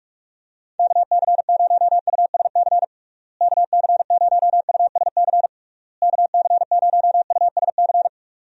Morse code, heard as a keyed tone, about 46 wpm, 700 Hz. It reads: KC0USC KC0USC KC0USC
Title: USA Call Signs - 6 Characters - Repeated 3x - 45wpm